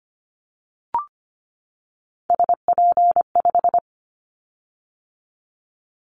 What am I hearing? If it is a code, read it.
Morse code, heard as a keyed tone, about 25 wpm, 700 Hz. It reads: SP5
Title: Callsign Prefixes for Top 25 DXCC - 25wpm